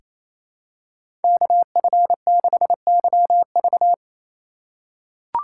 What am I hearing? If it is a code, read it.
Morse code, heard as a keyed tone, about 28 wpm, 700 Hz. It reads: KF6YV
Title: USA Call Signs - 5 Characters - 4x Character Spacing - 28wpm